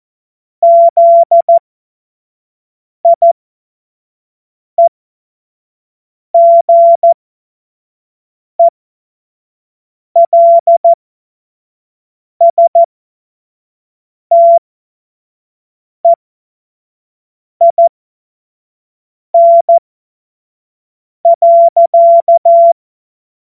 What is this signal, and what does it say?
Morse code, heard as a keyed tone, about 14 wpm, spaced as for 5 wpm, 680 Hz. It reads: ZIEGELSTEIN.
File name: Morsekurs / Witze / Witz 17. Mai 2022 25 F70